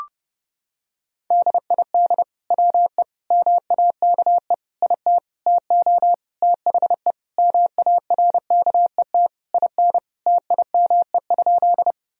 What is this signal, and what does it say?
Morse code, heard as a keyed tone, about 30 wpm, 700 Hz. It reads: DID WE MAKE IT TO THE MARKET IN TIME?
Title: Sentences from Top 1000 Words - Review Part 3 - 30wpm